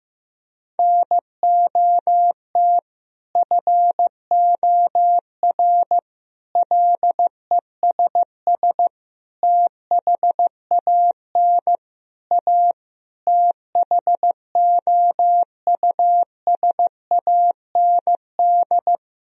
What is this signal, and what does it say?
Morse code, heard as a keyed tone, about 15 wpm, 700 Hz. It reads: NOT FOR LESS THAN A THOUSAND
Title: Sentences from Top 1000 Words - Review Part 3 - 15wpm